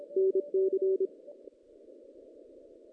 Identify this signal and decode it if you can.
Morse code, heard as a keyed tone, about 25 wpm, 410 Hz. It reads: NC
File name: 2016 JAN CW NAQP / 2016 Jan naqp k5ot - 0-5